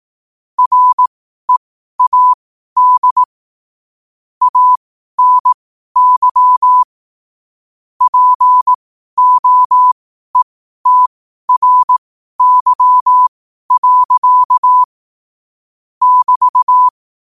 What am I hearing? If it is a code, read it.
Morse code, heard as a keyed tone, about 18 wpm, spaced as for 12 wpm, 1000 Hz. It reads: READ ANY POETRY. <BT>
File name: qotdfiles / 12wpm